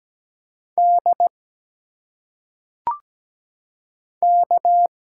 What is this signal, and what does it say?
Morse code, heard as a keyed tone, about 17 wpm, 700 Hz. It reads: D K
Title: Single Letters - Phonetic Pronunciation 17wpm